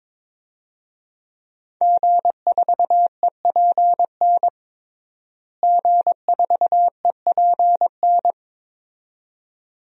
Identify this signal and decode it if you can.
Morse code, heard as a keyed tone, about 22 wpm, 700 Hz. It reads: G4EPN G4EPN
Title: DXCC-100 Call Signs - 5 Characters - Repeated 2x - 22wpm